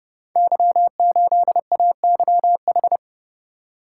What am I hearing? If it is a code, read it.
Morse code, heard as a keyed tone, about 30 wpm, 700 Hz. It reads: Y8AYH
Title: Code Groups - 5 Letter:Numbers - 30wpm